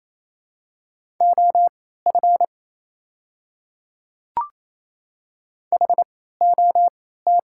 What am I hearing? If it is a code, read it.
Morse code, heard as a keyed tone, about 28 wpm, spaced as for 16 wpm, 700 Hz. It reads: OF HOT
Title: 3x Character Spacing - Top 200 Words - 28wpm